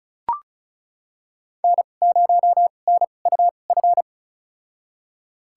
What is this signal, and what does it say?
Morse code, heard as a keyed tone, about 36 wpm, spaced as for 25 wpm, 700 Hz. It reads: N0NUF
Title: USA Call Signs - 5 Characters - 2x Character Spacing - 35wpm